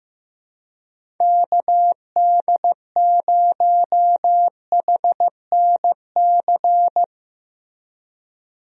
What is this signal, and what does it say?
Morse code, heard as a keyed tone, about 16 wpm, 700 Hz. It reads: KD0HNC
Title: USA Call Signs - 6 Characters - Repeated 2x - 15wpm